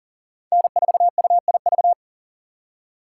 Morse code, heard as a keyed tone, about 37 wpm, 700 Hz. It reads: N4UIV
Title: USA Call Signs - 5 Characters - 3x Character Spacing - 40wpm